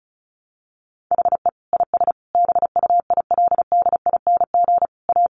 Morse code, heard as a keyed tone, about 34 wpm, 700 Hz. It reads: HE IS BUILDING A
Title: Sentences from Top 700 Words 35wpm